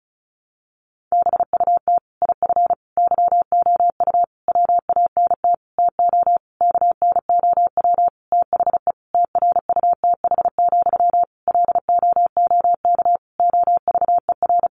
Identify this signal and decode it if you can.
Morse code, heard as a keyed tone, about 34 wpm, 700 Hz. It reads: BUT IF YOU WANT TO KNOW THE TRUTH, LOOK OVER
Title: Sentences from Top 800 Words 35wpm